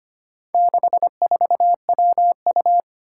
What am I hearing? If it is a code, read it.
Morse code, heard as a keyed tone, about 25 wpm, 700 Hz. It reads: 64WU
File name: practice / 25wpm / Four Letter-Number Combinations - 25wpm